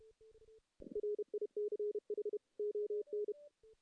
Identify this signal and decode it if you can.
Morse code, heard as a keyed tone, about 32 wpm, 380 Hz. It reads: RICH ON
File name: 2016 JAN CW NAQP / 2016 Jan naqp k5ot - 0-2